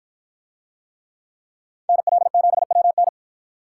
Morse code, heard as a keyed tone, about 50 wpm, 700 Hz. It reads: NL7WN